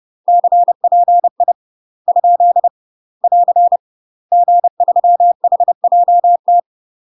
Morse code, heard as a keyed tone, about 30 wpm, 700 Hz. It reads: CPI ? <AR> G3HJT
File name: UK MASTER QSOs TRAINING / 30WPM / UK MASTER QSOs 0042